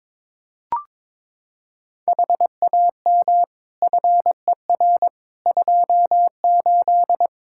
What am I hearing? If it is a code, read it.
Morse code, heard as a keyed tone, about 22 wpm, 700 Hz. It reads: HAM FER 28
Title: Practice QSO - Exchange 3 - More Info - 22wpm